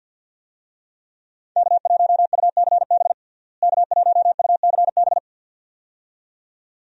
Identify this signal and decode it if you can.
Morse code, heard as a keyed tone, about 52 wpm, 700 Hz. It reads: K1UCB K1UCB
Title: USA Call Signs - 5 Characters - Repeated 2x - 50wpm